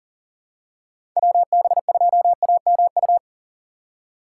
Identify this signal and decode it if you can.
Morse code, heard as a keyed tone, about 40 wpm, 700 Hz. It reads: WB2AMU
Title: USA Call Signs - 4x Character Spacing - 40wpm